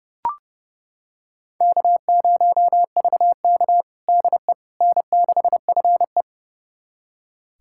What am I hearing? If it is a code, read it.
Morse code, heard as a keyed tone, about 30 wpm, 700 Hz. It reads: K0VK DE N6FE
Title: USA Callsign DE Callsign - 4 Characters - 30wpm